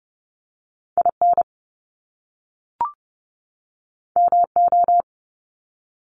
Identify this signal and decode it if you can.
Morse code, heard as a keyed tone, about 30 wpm, 700 Hz. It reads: IN MO
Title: US States 30wpm